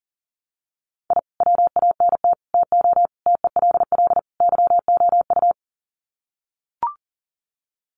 Morse code, heard as a keyed tone, about 40 wpm, 700 Hz. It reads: I WANT TO TELL YOU
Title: Sets of 5 Words Encore 40wpm